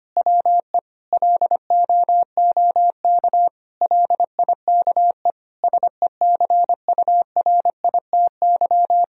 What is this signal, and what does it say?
Morse code, heard as a keyed tone, about 25 wpm, 700 Hz. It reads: WE LOOK LIKE SECURITY
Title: Sentences from Top 1000 Words - Review Part 4 - 25wpm